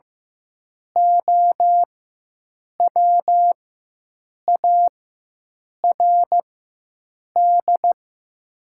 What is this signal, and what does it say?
Morse code, heard as a keyed tone, about 15 wpm, spaced as for 7 wpm, 700 Hz. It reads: OWARD
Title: 4x Character Spacing - Top 500 Words - 15wpm